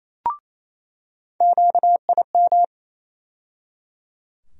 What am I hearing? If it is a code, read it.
Morse code, heard as a keyed tone, about 28 wpm, 700 Hz. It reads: QIM
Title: USA Call Sign Postfixes - 3 Letters 28wpm